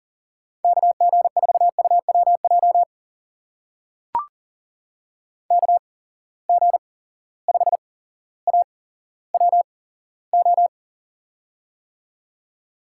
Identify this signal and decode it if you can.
Morse code, heard as a keyed tone, about 42 wpm, spaced as for 17 wpm, 700 Hz. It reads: KG4UWJ KG5AWO
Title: USA Call Signs - 6 Characters - 8x Character Spacing - 40wpm